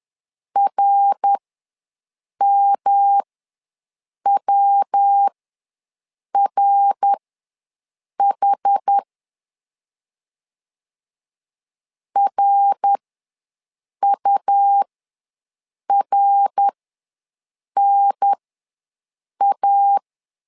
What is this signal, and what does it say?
Morse code, heard as a keyed tone, about 11 wpm, spaced as for 6 wpm, 800 Hz. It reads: RMWRH RURNA
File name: LEZIONI / 3 MWRO / 3 MWR ws 10wpm 2151